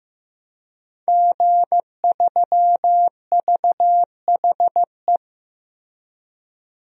Morse code, heard as a keyed tone, about 15 wpm, 700 Hz. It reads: G3VHE
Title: DXCC-100 Call Signs - 5 Characters - Repeated 2x - 15wpm